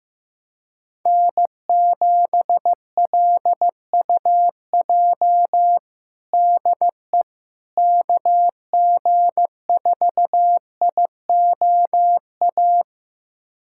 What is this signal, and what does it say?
Morse code, heard as a keyed tone, about 15 wpm, 700 Hz. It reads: N7LUJ DE KG4IOA
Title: USA Callsign DE Callsign 15wpm